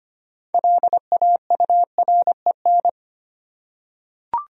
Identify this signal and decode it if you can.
Morse code, heard as a keyed tone, about 25 wpm, 700 Hz. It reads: LAUREN
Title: USA Common Names - 25wpm